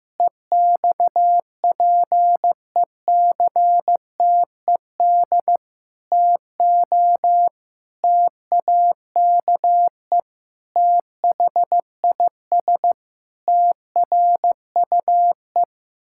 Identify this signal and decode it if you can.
Morse code, heard as a keyed tone, about 16 wpm, 700 Hz. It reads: EXPECTED TO TAKE THIS TRUE
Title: Pangram Challenge v2 - 15wpm